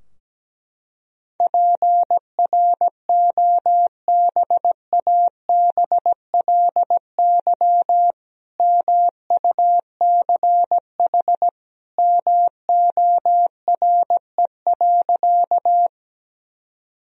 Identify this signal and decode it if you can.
Morse code, heard as a keyed tone, about 17 wpm, 700 Hz. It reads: PROBABLY MUCH MORE.
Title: Sentences from Top 1000 Words - Review Part 4 - 17wpm